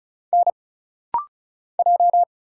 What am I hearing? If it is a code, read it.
Morse code, heard as a keyed tone, about 34 wpm, 700 Hz. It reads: N J
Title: Single Letter-Number Alt - Rapid-Fire 35wpm